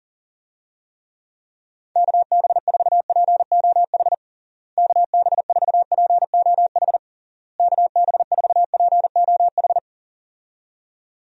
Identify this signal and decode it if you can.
Morse code, heard as a keyed tone, about 40 wpm, 700 Hz. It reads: KB4POH KB4POH KB4POH
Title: USA Call Signs - 6 Characters - Repeated 3x - 40wpm